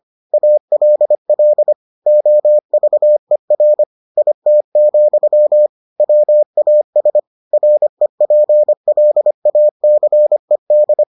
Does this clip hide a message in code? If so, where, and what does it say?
Yes: Morse code from 0.3 to 11.0 s, heard as a keyed tone, about 25 wpm, 600 Hz. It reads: ALL OVER IT, WAS REPLACED